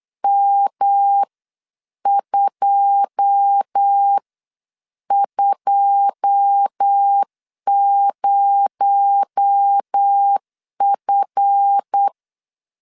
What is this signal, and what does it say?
Morse code, heard as a keyed tone, about 8 wpm, spaced as for 7 wpm, 800 Hz. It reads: M220F
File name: LEZIONI / 9 2 8wpm 3437